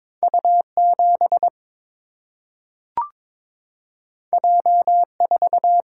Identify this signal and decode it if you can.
Morse code, heard as a keyed tone, about 22 wpm, 700 Hz. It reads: U 7 J 4
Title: Single-Letter Single-Number 22wpm